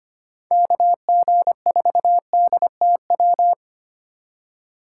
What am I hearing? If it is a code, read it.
Morse code, heard as a keyed tone, about 25 wpm, 700 Hz. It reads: KG4DTW
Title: USA Call Signs - 6 Characters - 4x Character Spacing - 25wpm